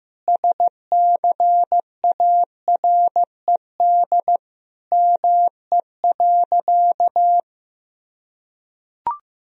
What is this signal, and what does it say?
Morse code, heard as a keyed tone, about 15 wpm, 700 Hz. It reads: SCARED ME.
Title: Sentences from Top 1000 Words - Review Part 4 - 15wpm